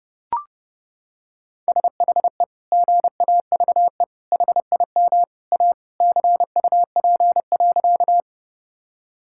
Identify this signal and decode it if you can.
Morse code, heard as a keyed tone, about 30 wpm, 700 Hz. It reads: SHE GAVE HIM A CUP.
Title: Sentences from Top 1600 Words 30wpm